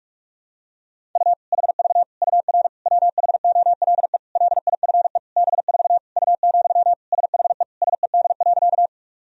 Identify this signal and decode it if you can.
Morse code, heard as a keyed tone, about 46 wpm, 700 Hz. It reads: U HV UR WHOLE LIFE B4 U, SHE SED.
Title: Sentences from Top 400 Words and Abbreviations 45wpm